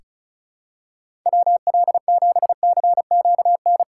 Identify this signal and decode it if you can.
Morse code, heard as a keyed tone, about 36 wpm, 700 Hz. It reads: WL7CQN
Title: USA Call Signs - 4x Character Spacing - 35wpm